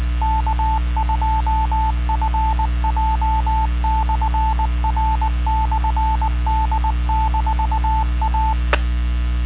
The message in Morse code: K2FJ/R/D-A